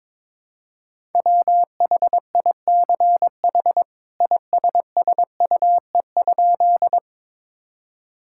Morse code, heard as a keyed tone, about 22 wpm, 700 Hz. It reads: WHICH ISSUE?
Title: Sentences from Top 1000 Words - Review Part 6 - 22wpm